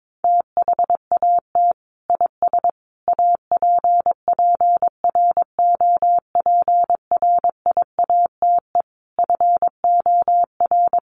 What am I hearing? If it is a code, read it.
Morse code, heard as a keyed tone, about 22 wpm, 700 Hz. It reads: THAT IS APPROPRIATE FOR
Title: Sentences from Top 900 Words 22wpm